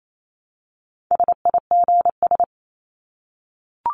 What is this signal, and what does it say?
Morse code, heard as a keyed tone, about 28 wpm, 700 Hz. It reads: SIGS
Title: Abbreviations 28wpm